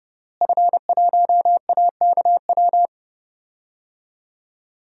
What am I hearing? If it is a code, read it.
Morse code, heard as a keyed tone, about 30 wpm, 700 Hz. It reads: F1AKW
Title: DXCC-100 Call Signs - 5 Characters - Repeated 3x - 30wpm